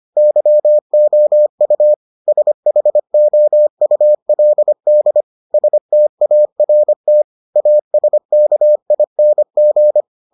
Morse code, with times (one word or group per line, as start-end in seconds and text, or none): 0.2-1.9 YOU
2.3-5.2 SHOULD
5.5-7.2 START
7.6-10.0 ASKING